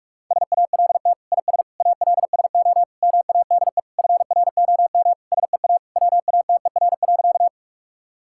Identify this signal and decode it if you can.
Morse code, heard as a keyed tone, about 43 wpm, 700 Hz. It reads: SALT IS ALSO MADE FROM SEA WATER.
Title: Sentences from Top 1000 Words - Review Part 5 - 45wpm